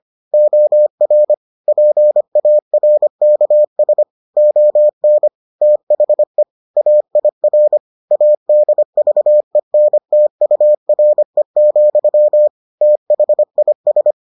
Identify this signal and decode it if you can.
Morse code, heard as a keyed tone, about 25 wpm, 600 Hz. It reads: OR PARKS ON THE AIR ADVENTURE, THIS